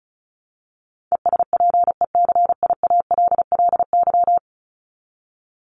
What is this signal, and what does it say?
Morse code, heard as a keyed tone, about 34 wpm, 700 Hz. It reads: ESPECIALLY
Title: Top 501-750 Words 35wpm